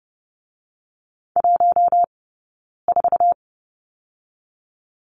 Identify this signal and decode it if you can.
Morse code, heard as a keyed tone, about 30 wpm, 700 Hz. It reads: 1 4
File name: 092 Lesson 16d - Letter c - Two Characters 30wpm